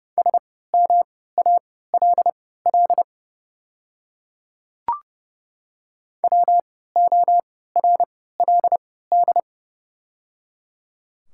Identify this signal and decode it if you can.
Morse code, heard as a keyed tone, about 29 wpm, spaced as for 17 wpm, 690 Hz. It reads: SMALL WORLD
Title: 3x Character Spacing - Top 200 Words - 30wpm